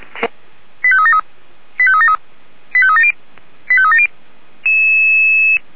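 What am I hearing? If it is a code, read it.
Five-tone selective call (ZVEI), heard as quick runs of five tones, tones of about 70 ms. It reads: 85373 85373 85370 85370
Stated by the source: ZVEI five-tone call received by SDR, from the feuerwehr-orlamuende.org fire brigade audio server, station f85360.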